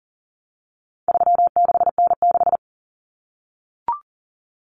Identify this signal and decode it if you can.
Morse code, heard as a keyed tone, about 40 wpm, 700 Hz. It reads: 36N6
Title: Common Cut Numbers - 4-Digit 40wpm